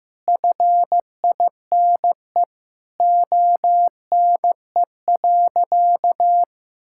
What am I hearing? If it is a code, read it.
Morse code, heard as a keyed tone, about 15 wpm, 700 Hz. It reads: FINE ONE.
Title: Sentences from Top 1000 Words - Review Part 6 - 15wpm